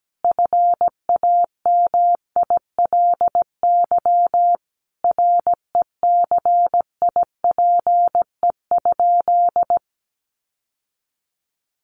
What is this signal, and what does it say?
Morse code, heard as a keyed tone, about 17 wpm, 700 Hz. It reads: FAMILY RECIPE?
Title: Sentences from Top 1200 Words 17wpm